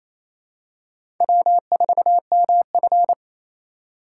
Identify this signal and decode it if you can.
Morse code, heard as a keyed tone, about 28 wpm, 700 Hz. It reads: W4MF
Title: USA Call Signs - 4 Characters - 3x Character Spacing - 28wpm